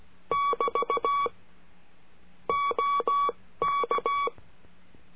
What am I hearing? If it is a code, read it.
Morse code, heard as a keyed tone, about 16 wpm, spaced as for 14 wpm, 1130 Hz. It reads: <BT> OK
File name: CURSO DE TELEGRAFIA